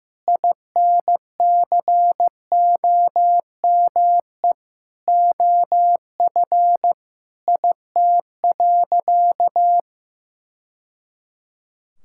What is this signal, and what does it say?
Morse code, heard as a keyed tone, about 15 wpm, 700 Hz. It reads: INCOME OF IT.
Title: Sentences from Top 1000 Words - Review Part 6 - 15wpm